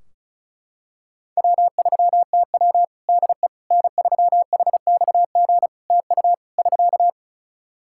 Morse code, heard as a keyed tone, about 36 wpm, 700 Hz. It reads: W3TW DE N3HXG TU <SK>
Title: Practice QSO - Exchange 4 - Ending - 35wpm